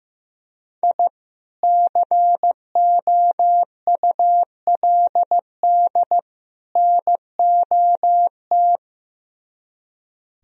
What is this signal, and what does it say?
Morse code, heard as a keyed tone, about 17 wpm, 700 Hz. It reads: I COULD NOT
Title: Speed-Racing - Sets of 3 Words 15wpm